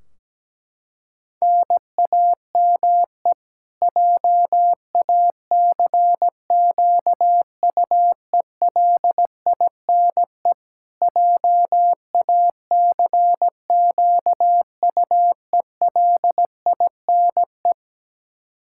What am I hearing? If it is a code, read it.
Morse code, heard as a keyed tone, about 17 wpm, 700 Hz. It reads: NAME JACQUELINE JACQUELINE
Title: Practice QSO - Exchange 1 - Beginning - 17wpm